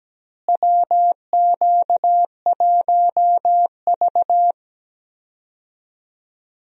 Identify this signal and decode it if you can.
Morse code, heard as a keyed tone, about 17 wpm, 700 Hz. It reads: WQ1V
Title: USA Call Signs - 4 Characters - Repeated 3x - 17wpm